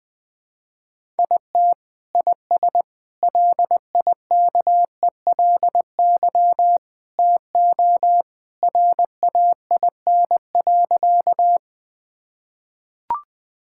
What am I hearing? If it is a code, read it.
Morse code, heard as a keyed tone, about 20 wpm, 700 Hz. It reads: IT IS LIKELY TO RAIN.
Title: Sentences with 2-Word Phrases - 20wpm